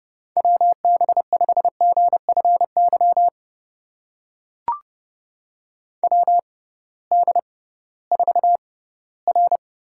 Morse code, heard as a keyed tone, about 30 wpm, spaced as for 18 wpm, 700 Hz. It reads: WB5GFY WD4R
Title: USA Call Signs - 6 Characters - 6x Character Spacing - 30wpm